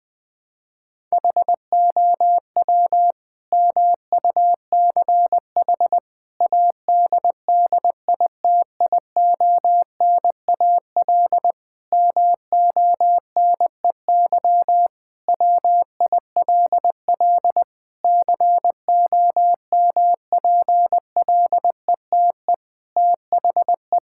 Morse code, heard as a keyed tone, about 20 wpm, 700 Hz. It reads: HOW MUCH ADDITIONAL MONEY WILL COMPLETE THE
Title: Sentences from Top 1000 Words - Review Part 5 - 20wpm